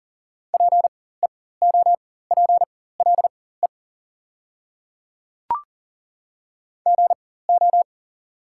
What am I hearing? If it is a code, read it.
Morse code, heard as a keyed tone, about 40 wpm, spaced as for 19 wpm, 690 Hz. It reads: PEOPLE GO
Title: 4x Character Spacing - Top 500 Words - 40wpm